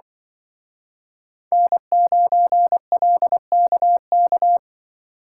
Morse code, heard as a keyed tone, about 25 wpm, 700 Hz. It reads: N9LKK
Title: Speed Racing - USA Call Signs 20wpm